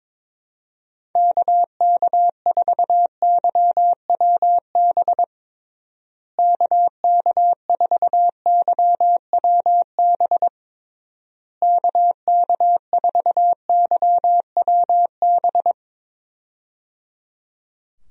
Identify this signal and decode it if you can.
Morse code, heard as a keyed tone, about 21 wpm, 700 Hz. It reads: KK4YWB KK4YWB KK4YWB
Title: USA Call Signs - 6 Characters - Repeated 3x - 22wpm